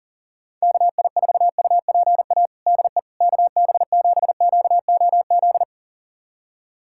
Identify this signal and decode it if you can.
Morse code, heard as a keyed tone, about 37 wpm, 700 Hz. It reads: KI4UPA DE KB7QOZ
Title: USA Callsign DE Callsign 40wpm